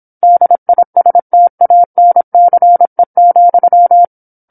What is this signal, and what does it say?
Morse code, heard as a keyed tone, about 27 wpm, 700 Hz. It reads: DISTANCE,
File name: 08-Chapter-08 26 WPM